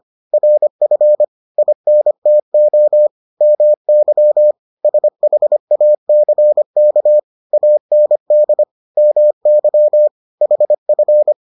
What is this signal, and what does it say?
Morse code, heard as a keyed tone, about 25 wpm, 600 Hz. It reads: RF INTO MY SHACK AND MY HF